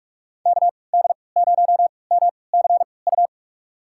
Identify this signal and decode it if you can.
Morse code, heard as a keyed tone, about 47 wpm, spaced as for 26 wpm, 700 Hz. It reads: KD0MCU
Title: USA Call Signs - 6 Characters - 3x Character Spacing - 45wpm